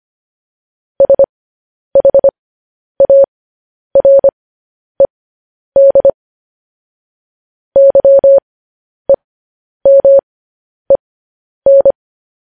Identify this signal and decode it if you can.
Morse code, heard as a keyed tone, about 25 wpm, spaced as for 10 wpm, 550 Hz. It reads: SHARED YEMEN